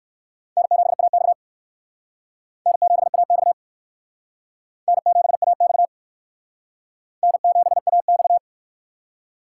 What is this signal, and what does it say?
Morse code, heard as a keyed tone, about 58 wpm, 700 Hz. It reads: N7AX N7AX N7AX N7AX